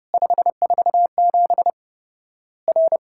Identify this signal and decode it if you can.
Morse code, heard as a keyed tone, about 32 wpm, 700 Hz. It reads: 547 R
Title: Worked All Europe DX Contest - QTC Exchange - 30wpm